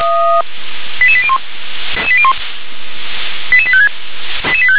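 Five-tone selective call (ZVEI), heard as quick runs of five tones, tones of about 70 ms. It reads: 88801 88856
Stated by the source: ZVEI five-tone selective call received by SDR.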